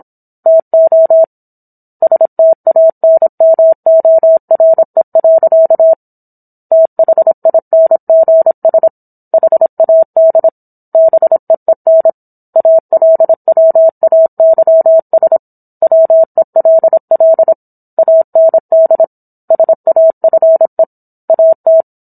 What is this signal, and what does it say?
Morse code, heard as a keyed tone, about 27 wpm, 650 Hz. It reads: TO STANMORE. THINGS HAD BEEN ALWAYS WELL AND SAFE AT